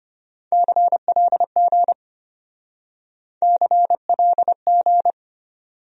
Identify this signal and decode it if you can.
Morse code, heard as a keyed tone, about 32 wpm, 700 Hz. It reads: CLG CLG